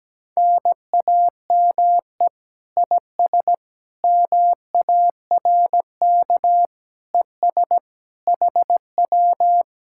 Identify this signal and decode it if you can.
Morse code, heard as a keyed tone, about 17 wpm, 700 Hz. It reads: NAME IS MARK ES HW
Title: Speed-Racing Common QSO Phrases Encore - 17wpm